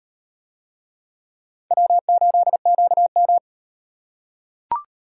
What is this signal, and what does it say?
Morse code, heard as a keyed tone, about 38 wpm, 700 Hz. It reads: W8QM